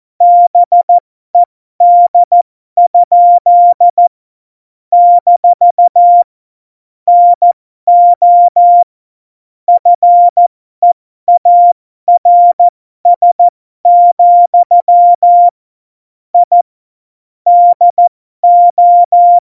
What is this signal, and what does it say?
Morse code, heard as a keyed tone, about 14 wpm, spaced as for 12 wpm, 700 Hz. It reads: BED? - NO FEARS, I DO
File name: Morsekurs / Witze